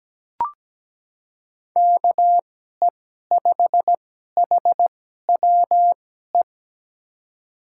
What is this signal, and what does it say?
Morse code, heard as a keyed tone, about 17 wpm, spaced as for 12 wpm, 700 Hz. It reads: KE5HWE
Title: USA Call Signs - 6 Characters - 2x Character Spacing - 17wpm